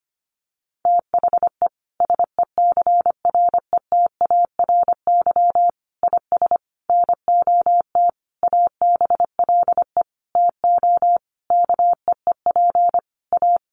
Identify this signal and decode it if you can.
Morse code, heard as a keyed tone, about 25 wpm, 700 Hz. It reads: THE SECRETARY IS NOT ABLE TO KEEP A
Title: Sentences from Top 1300 Words 25wpm